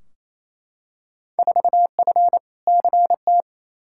Morse code, heard as a keyed tone, about 28 wpm, 700 Hz. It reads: 4F CT
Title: Field Day Exchanges Encore - 28wpm